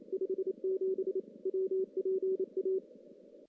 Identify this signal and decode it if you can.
Morse code, heard as a keyed tone, about 30 wpm, 380 Hz. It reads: 57 WPA
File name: contests / CW-SS-k5tr- -1600